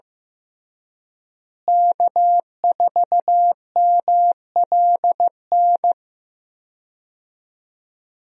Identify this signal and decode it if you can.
Morse code, heard as a keyed tone, about 15 wpm, 700 Hz. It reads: K4MLN
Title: USA Call Signs - 5 Characters - Repeated 3x - 15wpm